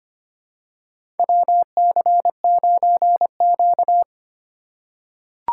Morse code, heard as a keyed tone, about 25 wpm, 700 Hz. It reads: WC9Q
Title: USA Call Signs - 8x Character Spacing - 25wpm